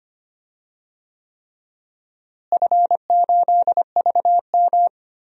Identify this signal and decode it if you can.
Morse code, heard as a keyed tone, about 25 wpm, 700 Hz. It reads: F8VM
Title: DXCC-100 Call Signs - 4 Characters - Repeated 3x - 25wpm